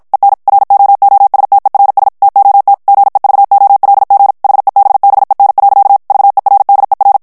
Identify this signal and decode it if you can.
Morse code, heard as a keyed tone, about 56 wpm, 800 Hz. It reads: EN ROOSTERS TOT GEVOLG HEBBEN. VERDER
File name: morse mp3 / HOLLANDAIS texte clair 2 vitesse 50 mots minute